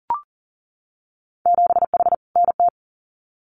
Morse code, heard as a keyed tone, about 40 wpm, 700 Hz. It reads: 7H NT
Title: Winter Field Day Exchanges Encore 40wpm